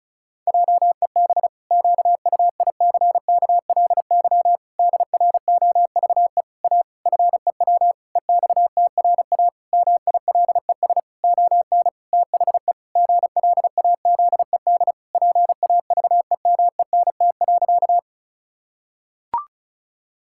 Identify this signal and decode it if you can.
Morse code, heard as a keyed tone, about 34 wpm, 700 Hz. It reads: JEB QUICKLY DROVE A FEW EXTRA MILES ON THE GLAZED PAVEMENT.
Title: Pangram Challenge v2 - 35wpm